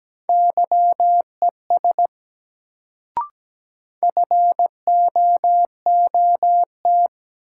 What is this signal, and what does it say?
Morse code, heard as a keyed tone, about 17 wpm, 700 Hz. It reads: YES FOOT
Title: Sentences from Top 500 Words - New Word Drill - 17wpm